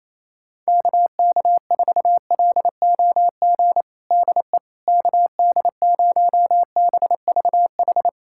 Morse code, heard as a keyed tone, about 27 wpm, 700 Hz. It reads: KK4LOG DE KD0BVH
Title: USA Callsign DE Callsign - 6 Characters - 28wpm